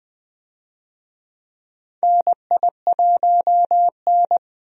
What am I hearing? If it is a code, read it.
Morse code, heard as a keyed tone, about 20 wpm, 700 Hz. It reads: NI1N
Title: USA Call Signs - 4 Characters - Repeated 3x - 20wpm